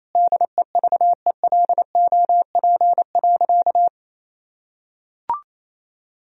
Morse code, heard as a keyed tone, about 28 wpm, 700 Hz. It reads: DEVELOP.
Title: Sentences from Top 1000 Words - Review Part 2 - 28wpm